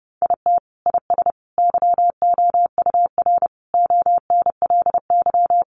Morse code, heard as a keyed tone, about 30 wpm, 700 Hz. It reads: IT IS YOUR ONLY
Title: Sentences from Top 1000 Words 30wpm